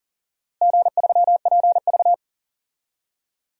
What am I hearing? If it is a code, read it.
Morse code, heard as a keyed tone, about 40 wpm, 700 Hz. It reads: G3PV
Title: DXCC-100 Call Signs - 4 Characters - Repeated 3x - 40wpm